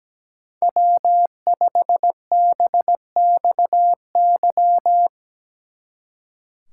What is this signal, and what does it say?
Morse code, heard as a keyed tone, about 17 wpm, 700 Hz. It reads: W5BXY
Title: Speed Racing - USA 1x3 Call Signs 17wpm